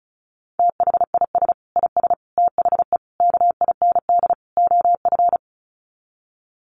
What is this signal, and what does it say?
Morse code, heard as a keyed tone, about 34 wpm, 700 Hz. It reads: THIS IS THE KIND OF
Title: Sets of 5 Words Encore 35wpm